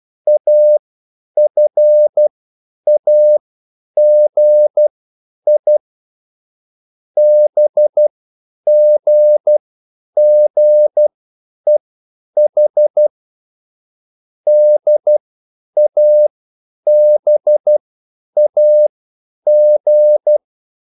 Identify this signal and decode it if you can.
Morse code, heard as a keyed tone, about 12 wpm, spaced as for 9 wpm, 600 Hz. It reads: AFAGI BGGEH DABAG